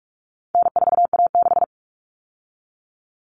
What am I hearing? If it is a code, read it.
Morse code, heard as a keyed tone, about 44 wpm, 700 Hz. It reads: N4A6
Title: Common Cut Numbers - 4-Digit 45wpm